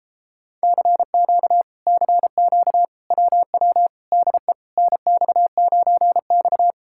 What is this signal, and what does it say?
Morse code, heard as a keyed tone, about 38 wpm, 700 Hz. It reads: CQ CQ WW DE NX9X